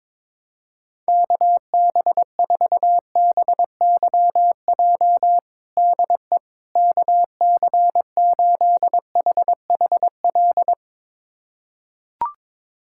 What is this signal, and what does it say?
Morse code, heard as a keyed tone, about 22 wpm, 700 Hz. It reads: KB4BYJ DE KC8HHL
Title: USA Callsign DE Callsign 22wpm